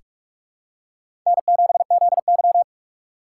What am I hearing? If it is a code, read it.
Morse code, heard as a keyed tone, about 45 wpm, 700 Hz. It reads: N7ZY
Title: USA Call Signs - 4x Character Spacing - 45wpm